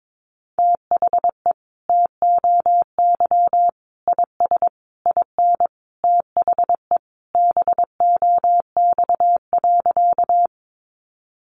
Morse code, heard as a keyed tone, about 22 wpm, 700 Hz. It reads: THE TOY IS IN THE BOX.
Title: Easy Phrases 22wpm